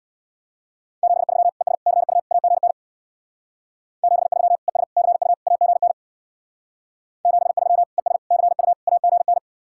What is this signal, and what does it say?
Morse code, heard as a keyed tone, about 64 wpm, 700 Hz. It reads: 73 ES CU AGN 73 ES CU AGN 73 ES CU AGN